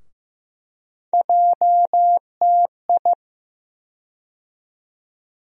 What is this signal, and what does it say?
Morse code, heard as a keyed tone, about 16 wpm, 700 Hz. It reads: JTI
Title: USA Call Sign Postfixes - 3 Letters 15wpm